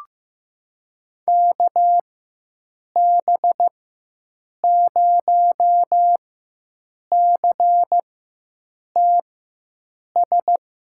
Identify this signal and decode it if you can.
Morse code, heard as a keyed tone, about 16 wpm, spaced as for 7 wpm, 700 Hz. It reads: KB0CTS
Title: USA Call Signs - 4x Character Spacing - 15wpm